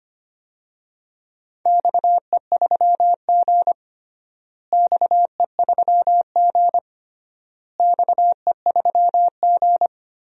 Morse code, heard as a keyed tone, about 25 wpm, 700 Hz. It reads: XE3G XE3G XE3G
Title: DXCC-100 Call Signs - 4 Characters - Repeated 3x - 25wpm